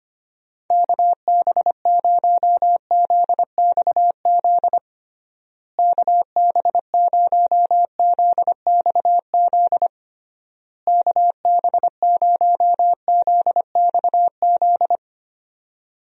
Morse code, heard as a keyed tone, about 25 wpm, 700 Hz. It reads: KB0ZXZ KB0ZXZ KB0ZXZ
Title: USA Call Signs - 6 Characters - Repeated 3x - 25wpm